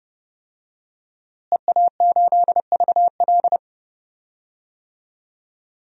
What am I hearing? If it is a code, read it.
Morse code, heard as a keyed tone, about 30 wpm, 700 Hz. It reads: EA8VL
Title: DXCC-100 Call Signs - 5 Characters - Repeated 3x - 30wpm